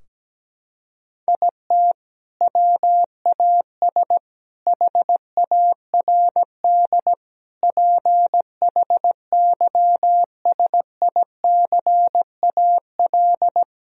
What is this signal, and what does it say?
Morse code, heard as a keyed tone, about 17 wpm, 700 Hz. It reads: IT WAS HARD PHYSICAL
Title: Sentences from Top 1000 Words - Review Part 4 - 17wpm